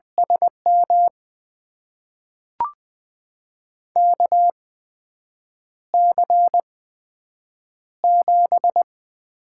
Morse code, heard as a keyed tone, about 20 wpm, spaced as for 7 wpm, 700 Hz. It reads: SM KC7
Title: USA Call Signs - 8x Character Spacing - 20wpm